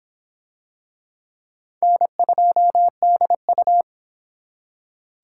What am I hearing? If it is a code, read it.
Morse code, heard as a keyed tone, about 22 wpm, 700 Hz. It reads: N2DU